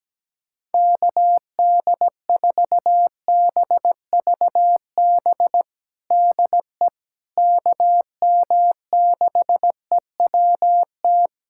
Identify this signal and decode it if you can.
Morse code, heard as a keyed tone, about 17 wpm, 700 Hz. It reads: KD4BVB DE KM6EWT
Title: USA Callsign DE Callsign - 6 Characters - 17wpm